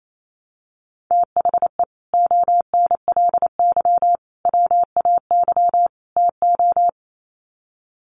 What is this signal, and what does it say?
Morse code, heard as a keyed tone, about 28 wpm, 700 Hz. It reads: THE ONLY WAY TO
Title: Sets of 4 Words 28wpm